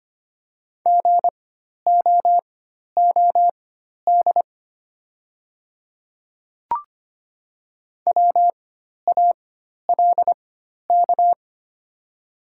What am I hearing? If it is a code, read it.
Morse code, heard as a keyed tone, about 25 wpm, spaced as for 12 wpm, 700 Hz. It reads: GOOD WALK
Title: 4x Character Spacing - Top 300 Words - 25wpm